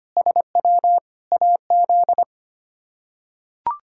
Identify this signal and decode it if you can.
Morse code, heard as a keyed tone, about 25 wpm, 700 Hz. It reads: SW AZ
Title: K1USN SST QSO Practice - 25wpm